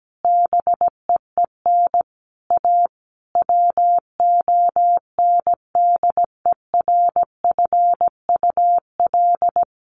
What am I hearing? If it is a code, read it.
Morse code, heard as a keyed tone, about 17 wpm, 700 Hz. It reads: BEEN A WONDERFUL
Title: Sentences from Top 1100 Words 17wpm